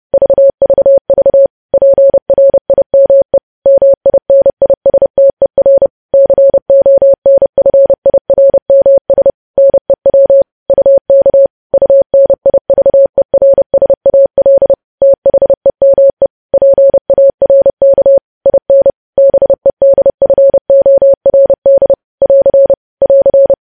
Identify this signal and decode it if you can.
Morse code, heard as a keyed tone, about 30 wpm, 550 Hz. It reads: VVV PRIME MINISTER CONFIRMS NEW UK UNIVERSAL THEME PARK IN BEDFORD <AR> <AR>
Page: news headlines in morse code at 30 wpm